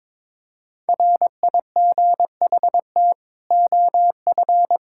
Morse code, heard as a keyed tone, about 21 wpm, 700 Hz. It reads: RIGHT OF
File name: Sets of 9 Words - 22wpm